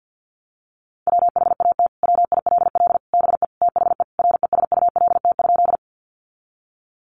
Morse code, heard as a keyed tone, about 50 wpm, 700 Hz. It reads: WHAT WILL BE THE RESULT?
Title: Sentences from Top 700 Words 50wpm